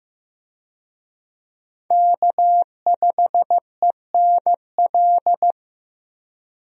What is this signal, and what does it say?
Morse code, heard as a keyed tone, about 15 wpm, 700 Hz. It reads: K5ENL
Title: USA Call Signs - 5 Characters - Repeated 3x - 15wpm